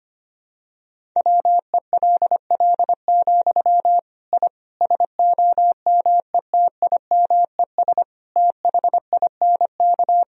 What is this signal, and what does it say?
Morse code, heard as a keyed tone, about 25 wpm, 700 Hz. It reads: WELL, I SOMETIMES THINK
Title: Sentences from Top 1000 Words - Review Part 3 - 25wpm